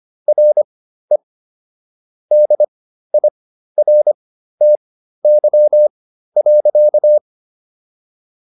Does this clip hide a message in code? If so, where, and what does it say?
Yes: Morse code from 0.3 to 7.2 s, heard as a keyed tone, about 25 wpm, spaced as for 13 wpm, 600 Hz. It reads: RE DIRTY.